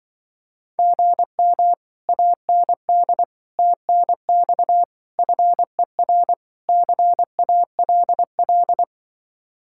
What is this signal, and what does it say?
Morse code, heard as a keyed tone, about 24 wpm, 700 Hz. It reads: GM AND TNX FER CALL
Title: Speed-Racing Common QSO Phrases Encore - 20wpm